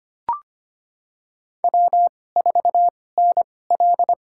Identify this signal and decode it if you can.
Morse code, heard as a keyed tone, about 25 wpm, spaced as for 18 wpm, 700 Hz. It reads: W4NL
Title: USA Call Signs - 2x Character Spacing - 25wpm